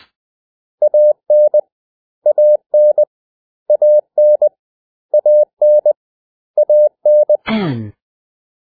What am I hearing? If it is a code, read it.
Morse code, heard as a keyed tone, about 20 wpm, 600 Hz. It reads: AN AN AN AN AN
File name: cw / 100 Common English Words